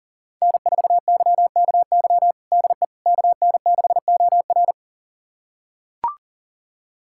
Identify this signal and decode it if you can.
Morse code, heard as a keyed tone, about 37 wpm, 700 Hz. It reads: N4YKY DE KN6OR
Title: USA Callsign DE Callsign 40wpm